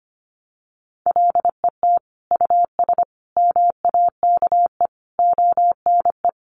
Morse code, heard as a keyed tone, about 25 wpm, 700 Hz. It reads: LET US MAKE ONE
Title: Sentences from Top 700 Words 25wpm